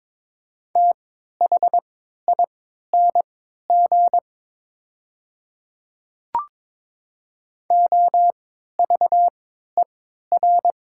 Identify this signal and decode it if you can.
Morse code, heard as a keyed tone, about 22 wpm, spaced as for 13 wpm, 690 Hz. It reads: THING OVER
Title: 3x Character Spacing - Top 100 Words - 22wpm